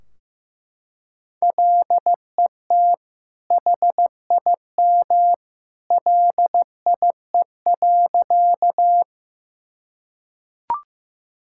Morse code, heard as a keyed tone, about 15 wpm, 700 Hz. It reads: LET HIM LIE.
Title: Sentences from Top 1600 Words 15wpm